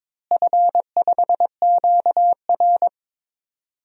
Morse code, heard as a keyed tone, about 21 wpm, 700 Hz. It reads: F5QR
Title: DXCC-100 Call Signs - 4 Characters - Repeated 2x - 22wpm